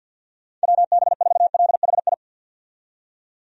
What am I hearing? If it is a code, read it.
Morse code, heard as a keyed tone, about 51 wpm, 700 Hz. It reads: WB4LHI
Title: USA Call Signs - 6 Characters - 50wpm